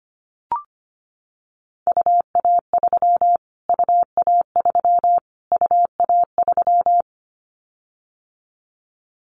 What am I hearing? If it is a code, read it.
Morse code, heard as a keyed tone, about 25 wpm, 700 Hz. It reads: UA3 UA3 UA3
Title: Callsign Prefixes for Top 25 DXCC - Repeated 3x - 25wpm